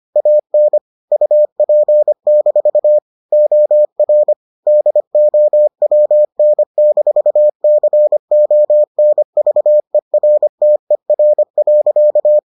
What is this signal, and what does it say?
Morse code, heard as a keyed tone, about 25 wpm, 600 Hz. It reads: AN UP- OR DOWN-CONVERTER.